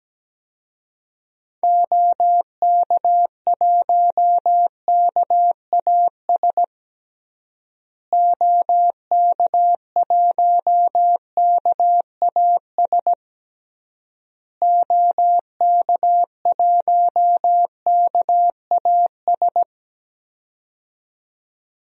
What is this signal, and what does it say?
Morse code, heard as a keyed tone, about 17 wpm, 700 Hz. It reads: OK1KAS OK1KAS OK1KAS
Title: DXCC-100 Call Signs - 6 Characters - Repeated 3x - 17wpm